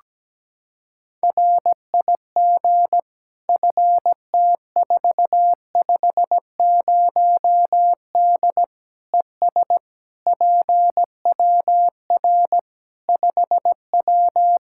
Morse code, heard as a keyed tone, about 17 wpm, 700 Hz. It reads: RIG FT450D ES PWR 5W
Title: Practice QSO - Exchange 2 - Info - 17wpm